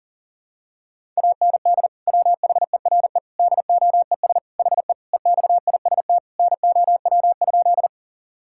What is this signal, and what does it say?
Morse code, heard as a keyed tone, about 40 wpm, 700 Hz. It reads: AND WHERE DOES HE EXIST NOW?
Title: Sentences from Top 1000 Words - Review Part 5 - 40wpm